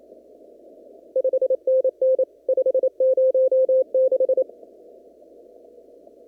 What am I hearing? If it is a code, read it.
Morse code, heard as a keyed tone, about 27 wpm, 520 Hz. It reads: 5NN 506
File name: CW / kt5j wpx cw-op-k5pi